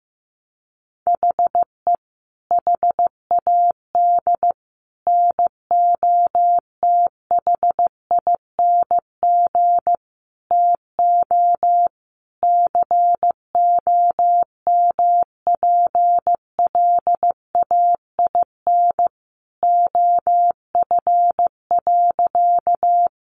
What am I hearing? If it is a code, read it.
Morse code, heard as a keyed tone, about 15 wpm, 700 Hz. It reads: HE HAD NOTHING TO COMPLAIN OF.
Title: Sentences from Top 1300 Words 15wpm